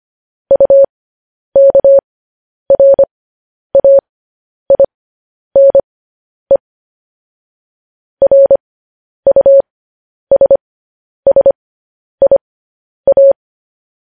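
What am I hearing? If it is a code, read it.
Morse code, heard as a keyed tone, about 25 wpm, spaced as for 10 wpm, 550 Hz. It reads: UKRAINE RUSSIA
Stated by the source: Morse code news headlines broadcast at 10 wpm.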